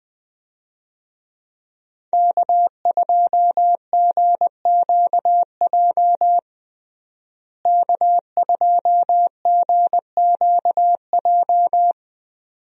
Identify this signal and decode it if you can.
Morse code, heard as a keyed tone, about 20 wpm, 700 Hz. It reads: K2GQJ K2GQJ
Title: USA Call Signs - 5 Characters - Repeated 2x - 20wpm